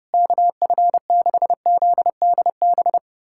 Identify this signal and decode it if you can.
Morse code, heard as a keyed tone, about 30 wpm, 700 Hz. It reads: KF6ZDB
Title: USA Callsign DE Callsign - 6 Characters - 30wpm